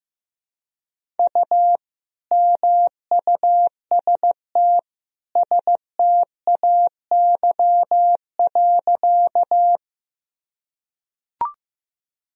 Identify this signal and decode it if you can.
Morse code, heard as a keyed tone, about 15 wpm, 700 Hz. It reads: U MUST STAY.
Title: Sentences from Top 500 Words and Abbreviations 15wpm